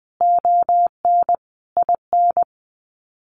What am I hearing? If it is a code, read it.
Morse code, heard as a keyed tone, about 20 wpm, 700 Hz. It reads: ON IN
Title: Sets of 5 Words Encore2 20wpm